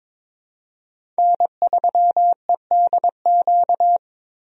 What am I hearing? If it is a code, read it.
Morse code, heard as a keyed tone, about 22 wpm, 700 Hz. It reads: N3EDQ
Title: USA Call Signs - 5 Characters - Repeated 2x - 22wpm